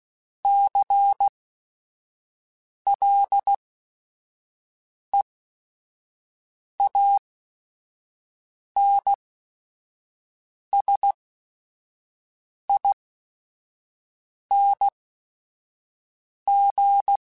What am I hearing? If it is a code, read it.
Morse code, heard as a keyed tone, about 16 wpm, spaced as for 5 wpm, 800 Hz. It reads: CLEANSING
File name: Special INGs / Special INGs-mp3s / Sngl Ing @5